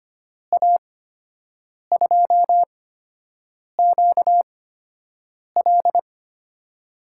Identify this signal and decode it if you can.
Morse code, heard as a keyed tone, about 25 wpm, spaced as for 7 wpm, 700 Hz. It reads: A2QL
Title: USA Call Signs - 5 Characters - 8x Character Spacing - 25wpm